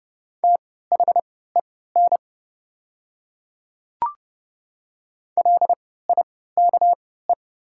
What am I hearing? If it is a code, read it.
Morse code, heard as a keyed tone, about 30 wpm, spaced as for 17 wpm, 690 Hz. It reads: THEN LIKE
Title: 3x Character Spacing - Top 200 Words - 30wpm